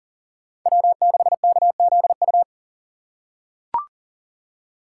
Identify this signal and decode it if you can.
Morse code, heard as a keyed tone, about 40 wpm, 700 Hz. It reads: W6KZU
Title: USA Call Signs - 5 Characters - 6x Character Spacing - 40wpm